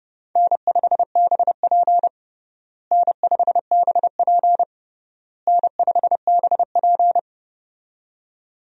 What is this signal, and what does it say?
Morse code, heard as a keyed tone, about 30 wpm, 700 Hz. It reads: N5BP N5BP N5BP
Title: USA Call Signs - 4 Characters - Repeated 3x - 30wpm